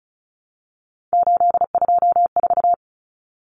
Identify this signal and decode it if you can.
Morse code, heard as a keyed tone, about 34 wpm, 700 Hz. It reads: 824
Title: Numbers - 3-Digit 35wpm